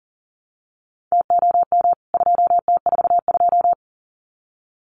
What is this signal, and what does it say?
Morse code, heard as a keyed tone, about 40 wpm, 700 Hz. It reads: TOM 2T42
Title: CWT Exchange 40wpm